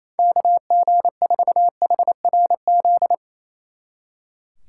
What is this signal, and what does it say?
Morse code, heard as a keyed tone, about 35 wpm, 700 Hz. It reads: KG4HRZ
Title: Speed Racing - USA Call Signs 28wpm